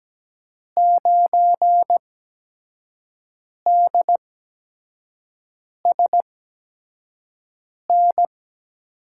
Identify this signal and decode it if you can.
Morse code, heard as a keyed tone, about 17 wpm, spaced as for 5 wpm, 700 Hz. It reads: 9DSN
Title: USA Call Signs - 5 Characters - 8x Character Spacing - 17wpm